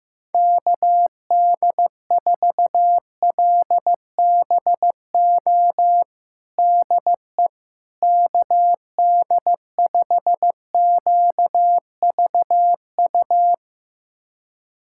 Morse code, heard as a keyed tone, about 16 wpm, 700 Hz. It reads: KD4LBO DE KD5QVU
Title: USA Callsign DE Callsign - 6 Characters - 15wpm